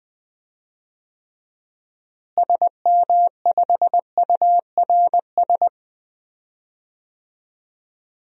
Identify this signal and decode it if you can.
Morse code, heard as a keyed tone, about 20 wpm, 700 Hz. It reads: SM5URS
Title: DXCC-100 Call Signs - 6 Characters - Repeated 3x - 20wpm